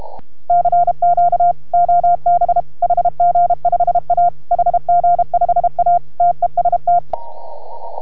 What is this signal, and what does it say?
Morse code, heard as a keyed tone, about 32 wpm, 690 Hz. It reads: CQ OB HG5A HG5A TEST